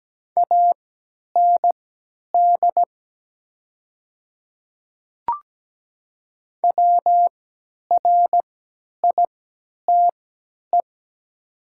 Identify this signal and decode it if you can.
Morse code, heard as a keyed tone, about 17 wpm, spaced as for 10 wpm, 700 Hz. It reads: AND WRITE
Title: 3x Character Spacing - Top 100 Words - 17wpm